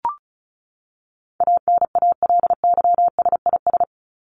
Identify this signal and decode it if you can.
Morse code, heard as a keyed tone, about 34 wpm, 700 Hz. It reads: ANALYSIS
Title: Top 501-750 Words 35wpm